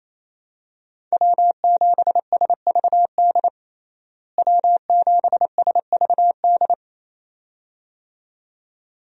Morse code, heard as a keyed tone, about 28 wpm, 700 Hz. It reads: W7SVD W7SVD
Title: USA Call Signs - 5 Characters - Repeated 2x - 28wpm